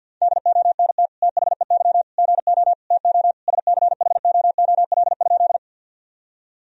Morse code, heard as a keyed tone, about 52 wpm, 700 Hz. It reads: DONT THEY GO TO SCHOOL?
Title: Sentences from Top 1000 Words - Review Part 1 - 50wpm